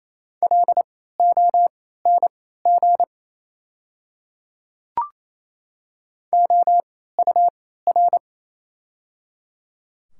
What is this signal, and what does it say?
Morse code, heard as a keyed tone, about 27 wpm, spaced as for 16 wpm, 700 Hz. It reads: LONG OUR
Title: 3x Character Spacing - Top 200 Words - 28wpm